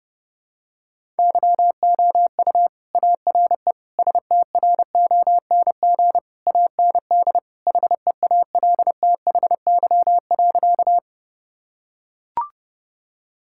Morse code, heard as a keyed tone, about 32 wpm, 700 Hz. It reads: YOU ARE STRONG AND HEALTHY.
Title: Sentences from Top 1000 Words - Review Part 5 - 30wpm